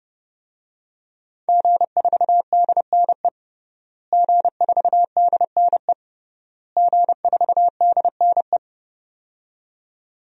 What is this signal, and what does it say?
Morse code, heard as a keyed tone, about 30 wpm, 700 Hz. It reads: G4DNE G4DNE G4DNE
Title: DXCC-100 Call Signs - 5 Characters - Repeated 3x - 30wpm